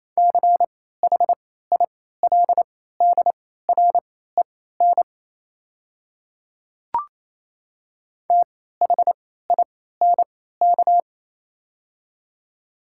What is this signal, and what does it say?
Morse code, heard as a keyed tone, about 28 wpm, spaced as for 16 wpm, 700 Hz. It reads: CHILDREN THINK
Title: 3x Character Spacing - Top 500 Words - 28wpm